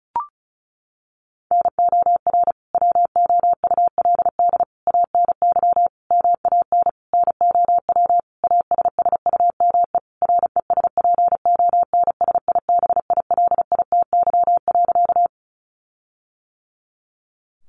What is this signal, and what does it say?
Morse code, heard as a keyed tone, about 34 wpm, 700 Hz. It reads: NOR WOULD ANY MAN NOW ASSUME RESPONSIBILITY.
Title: Sentences from Top 1500 Words 35wpm